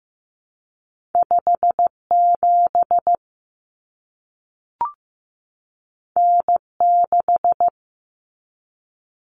Morse code, heard as a keyed tone, about 15 wpm, 700 Hz. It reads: 5 7 N 6
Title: Common Cut Numbers - 2-Digit 15wpm